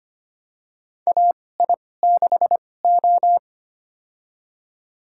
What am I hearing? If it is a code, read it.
Morse code, heard as a keyed tone, about 25 wpm, spaced as for 18 wpm, 700 Hz. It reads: AI6O
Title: USA Call Signs - 4 Characters - 2x Character Spacing - 25wpm